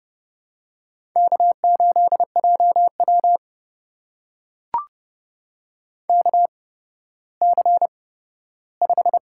Morse code, heard as a keyed tone, about 30 wpm, spaced as for 15 wpm, 700 Hz. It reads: K8JW KC5
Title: USA Call Signs - 8x Character Spacing - 30wpm